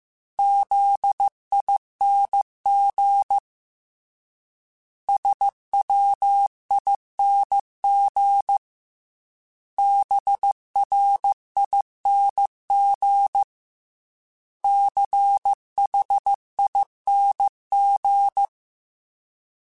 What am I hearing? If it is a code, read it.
Morse code, heard as a keyed tone, about 15 wpm, 800 Hz. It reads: ZING SWING BRING CHING
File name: Special INGs-mp3s / Special INGs-mp3s / Sngl ing @13